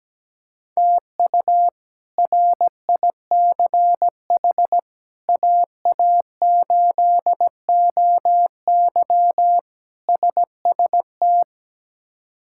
Morse code, heard as a keyed tone, about 17 wpm, 700 Hz. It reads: TU RICH AA8OY SST
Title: K1USN SST QSO Practice - 17wpm